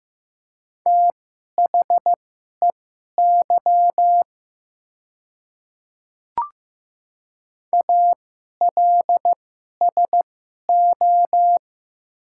Morse code, heard as a keyed tone, about 15 wpm, spaced as for 11 wpm, 700 Hz. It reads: THEY ALSO
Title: 2x Character Spacing - Top 500 Words - 15wpm